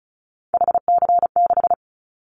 Morse code, heard as a keyed tone, about 34 wpm, 700 Hz. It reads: HC6
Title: Callsign Prefixes for Top 100 DXCC - 35wpm